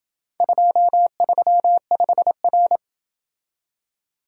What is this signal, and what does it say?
Morse code, heard as a keyed tone, about 26 wpm, 700 Hz. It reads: 235R